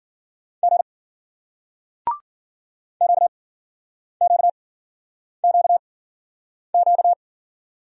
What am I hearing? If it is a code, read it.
Morse code, heard as a keyed tone, about 56 wpm, 700 Hz. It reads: G Q Q Q Q